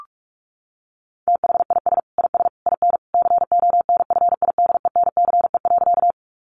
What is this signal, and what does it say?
Morse code, heard as a keyed tone, about 44 wpm, 700 Hz. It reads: THIS IS IN CONFIDENCE.
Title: Sentences from Top 1500 Words 45wpm